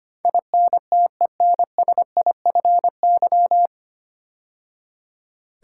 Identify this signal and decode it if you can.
Morse code, heard as a keyed tone, about 25 wpm, 700 Hz. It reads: INTENSIFY
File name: practice / 25wpm / 9 Letter Words - First 3 Letters Given - 25wpm